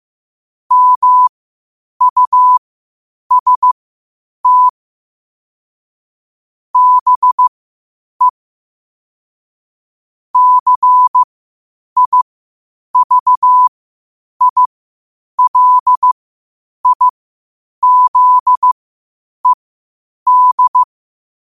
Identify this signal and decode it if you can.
Morse code, heard as a keyed tone, about 15 wpm, spaced as for 8 wpm, 1000 Hz. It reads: MUST BE CIVILIZED
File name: qotdfiles / 8wpm